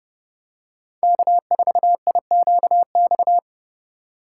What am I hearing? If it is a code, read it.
Morse code, heard as a keyed tone, about 30 wpm, 700 Hz. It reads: K4IQX
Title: USA Call Signs - 5 Characters - 3x Character Spacing - 30wpm